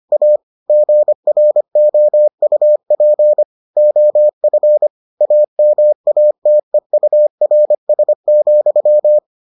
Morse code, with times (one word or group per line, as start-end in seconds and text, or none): 0.1-0.4 A
0.7-3.4 GROUP
3.8-4.9 OF
5.2-9.2 AMATEURS,